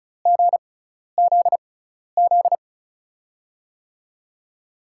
Morse code, heard as a keyed tone, about 36 wpm, spaced as for 12 wpm, 700 Hz. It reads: GZZ
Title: USA Call Signs - 5 Characters - 6x Character Spacing - 35wpm